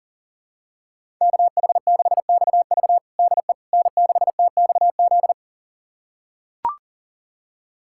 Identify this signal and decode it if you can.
Morse code, heard as a keyed tone, about 40 wpm, 700 Hz. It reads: KH6XV DE N6TXZ
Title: USA Callsign DE Callsign - 5 Characters - 40wpm